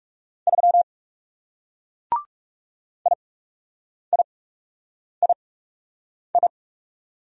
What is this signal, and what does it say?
Morse code, heard as a keyed tone, about 39 wpm, spaced as for 8 wpm, 700 Hz. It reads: 3 IIII